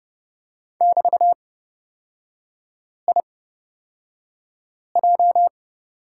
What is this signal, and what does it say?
Morse code, heard as a keyed tone, about 31 wpm, spaced as for 5 wpm, 700 Hz. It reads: <BT>IJ
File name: Lesson 38e - Letter q - Three Characters 30wpm